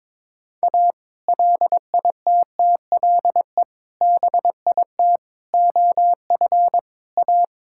Morse code, heard as a keyed tone, about 21 wpm, 700 Hz. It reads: A LITTLE BIT OF A
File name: practice / 22wpm / Build A Phrase - 5-Word Phrases - 22wpm